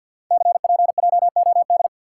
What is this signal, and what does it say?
Morse code, heard as a keyed tone, about 50 wpm, 700 Hz. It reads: CPJOD
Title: Code Groups - 5 Letters - 50wpm